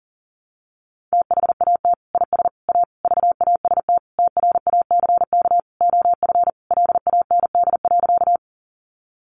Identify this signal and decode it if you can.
Morse code, heard as a keyed tone, about 40 wpm, 700 Hz. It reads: THAT IS A VAST TRACK OF LAND.
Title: Sentences from Top 1300 Words 40wpm